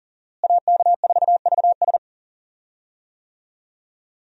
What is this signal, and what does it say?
Morse code, heard as a keyed tone, about 40 wpm, 700 Hz. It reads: AK4VS
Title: USA Call Signs - 5 Characters - Repeated 3x - 40wpm